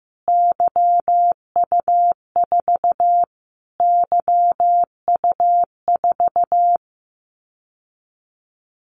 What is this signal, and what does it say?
Morse code, heard as a keyed tone, about 15 wpm, 700 Hz. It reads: YU4 YU4
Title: Callsign Prefixes for Top 25 DXCC - Repeated 3x - 15wpm